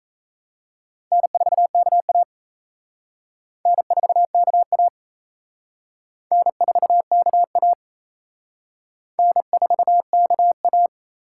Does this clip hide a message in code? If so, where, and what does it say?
Yes: Morse code from 1.1 to 10.9 s, heard as a keyed tone, about 37 wpm, 700 Hz. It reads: N4KA N4KA N4KA N4KA